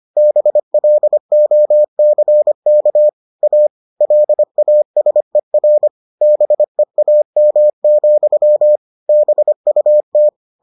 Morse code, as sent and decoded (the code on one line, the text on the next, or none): -... .-.. --- -.-. -.- / .- / .-.. .- ... .-. / -... .- -- --..-- / -... ..- -
BLOCK A LASER BEAM, BUT